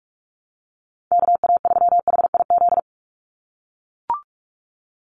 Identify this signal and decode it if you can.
Morse code, heard as a keyed tone, about 44 wpm, 700 Hz. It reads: KA3HIZ
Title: USA Call Signs 45wpm